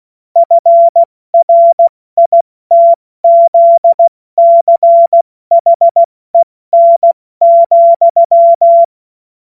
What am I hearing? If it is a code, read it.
Morse code, heard as a keyed tone, about 16 wpm, 680 Hz. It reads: FRITZCHEN,
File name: Witz-29-Juli-2025-70-F80